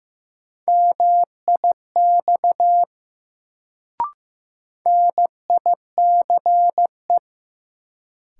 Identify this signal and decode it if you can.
Morse code, heard as a keyed tone, about 15 wpm, 700 Hz. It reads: MIX NICE
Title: Sentences from Top 1100 Words - New Word Drill Alt - 15wpm